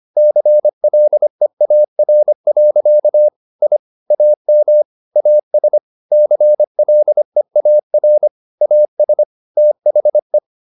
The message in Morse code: CLEAR. I AM AS CLEAR AS THE

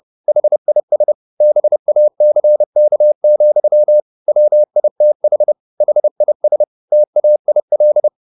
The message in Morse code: HIS BACK, WITH HIS TAIL

